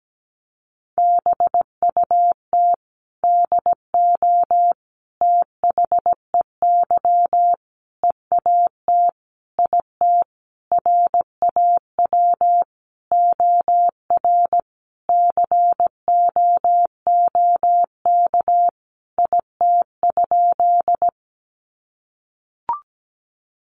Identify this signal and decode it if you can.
Morse code, heard as a keyed tone, about 17 wpm, 700 Hz. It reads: BUT DO THEY EAT IT RAW OR COOK IT?
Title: Sentences from Top 1500 Words 17wpm